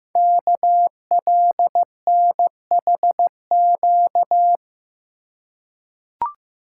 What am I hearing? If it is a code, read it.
Morse code, heard as a keyed tone, about 15 wpm, 700 Hz. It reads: KLNHQ
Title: Code Groups - 5 Letters - 15wpm